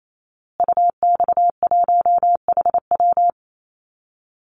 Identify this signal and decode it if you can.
Morse code, heard as a keyed tone, about 28 wpm, 700 Hz. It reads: UX1HW
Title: Call Signs Repeated 2x - 28wpm